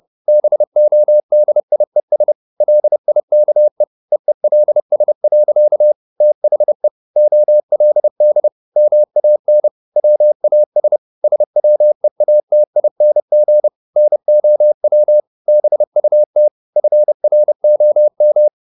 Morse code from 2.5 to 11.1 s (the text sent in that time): LIKE EELS. THE OLD MAN WAS